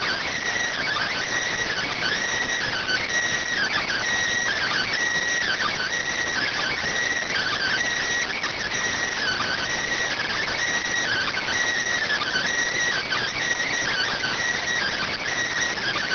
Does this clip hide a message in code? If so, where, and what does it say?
Yes: Emergency Alert System (SAME) tones (header bursts) from 1.0 to 15.1 s, heard as harsh buzzing bursts.